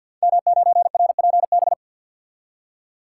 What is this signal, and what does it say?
Morse code, heard as a keyed tone, about 52 wpm, 700 Hz. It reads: M9RPB
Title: Code Groups - 5 Letter:Numbers - 50wpm